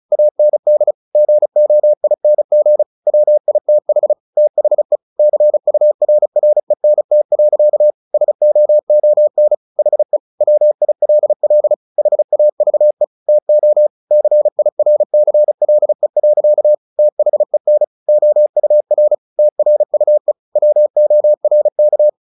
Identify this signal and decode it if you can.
Morse code, heard as a keyed tone, about 34 wpm, 600 Hz. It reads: AND GOING WITH THE CURRENT. SOON HE WILL HAVE TO CIRCLE. THEN OUR TRUE WORK
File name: the old man and the sea 2 35wpm